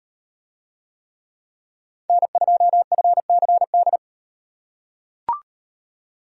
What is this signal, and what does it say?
Morse code, heard as a keyed tone, about 38 wpm, 700 Hz. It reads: N2FCD